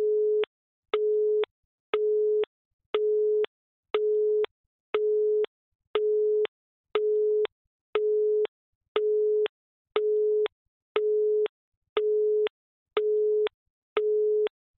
Busy tone (Europe, 425 Hz).
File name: NS Studios - ultra sound collection / devices accessories utilities / phone busy signal2